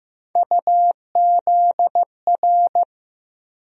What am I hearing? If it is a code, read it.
Morse code, heard as a keyed tone, about 15 wpm, 700 Hz. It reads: UZR
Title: USA Call Sign Postfixes - 3 Letters 15wpm